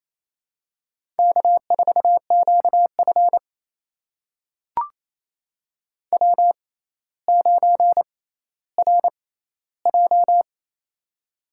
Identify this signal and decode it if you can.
Morse code, heard as a keyed tone, about 28 wpm, spaced as for 14 wpm, 700 Hz. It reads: K4QF W9RJ
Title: USA Call Signs - 4 Characters - 6x Character Spacing - 28wpm